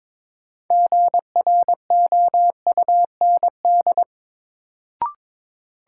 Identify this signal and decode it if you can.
Morse code, heard as a keyed tone, about 22 wpm, 700 Hz. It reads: GROUND
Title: Sets of 3 Words - New Word Drill - 22wpm